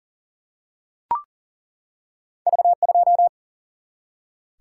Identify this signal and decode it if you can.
Morse code, heard as a keyed tone, about 39 wpm, 700 Hz. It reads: V2
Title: Single-Letter Single-Number 40wpm